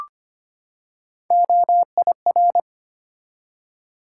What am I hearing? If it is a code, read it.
Morse code, heard as a keyed tone, about 25 wpm, 700 Hz. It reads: OIR
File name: Three Letter-Number Combinations - 25wpm